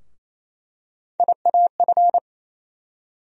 Morse code, heard as a keyed tone, about 28 wpm, 700 Hz. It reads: IAF
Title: USA Call Sign Postfixes - 3 Letters 28wpm